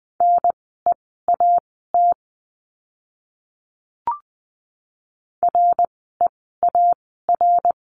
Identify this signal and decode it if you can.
Morse code, heard as a keyed tone, about 20 wpm, spaced as for 14 wpm, 700 Hz. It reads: NEAT REAR
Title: 054 Lesson 10f - Letter r - Words 20wpm